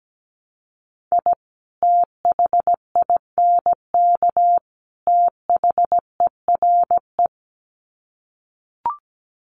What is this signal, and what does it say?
Morse code, heard as a keyed tone, about 17 wpm, 700 Hz. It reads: I THINK THERE
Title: Sets of 3 Words 17wpm